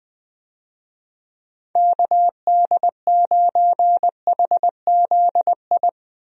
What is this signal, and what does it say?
Morse code, heard as a keyed tone, about 20 wpm, 700 Hz. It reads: KD9HZI
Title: USA Call Signs - 6 Characters - Repeated 2x - 20wpm